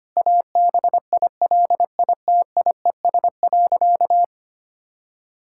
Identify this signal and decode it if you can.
Morse code, heard as a keyed tone, about 25 wpm, 700 Hz. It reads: ABILITIES.
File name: practice / 25wpm / Difficult Sentences with Hints - 25wpm